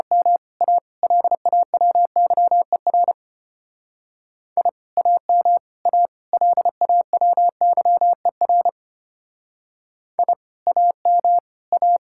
Morse code, heard as a keyed tone, about 31 wpm, 700 Hz. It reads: M A LAWYER I AM A LAWYER I AM A
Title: Speed-Racing Common QSO Phrases - 25wpm